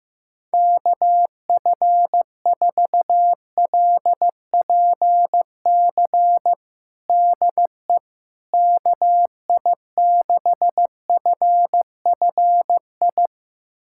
Morse code, heard as a keyed tone, about 15 wpm, 700 Hz. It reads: KF4LPC DE KI6FFI
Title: USA Callsign DE Callsign - 6 Characters - 15wpm